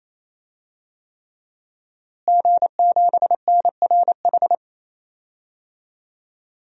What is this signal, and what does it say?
Morse code, heard as a keyed tone, about 28 wpm, 700 Hz. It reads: G7NRH
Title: DXCC-100 Call Signs - 5 Characters - Repeated 3x - 28wpm